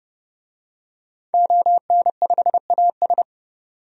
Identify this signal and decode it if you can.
Morse code, heard as a keyed tone, about 30 wpm, 700 Hz. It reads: ON5AS
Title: DXCC-100 Call Signs - 5 Characters - 30wpm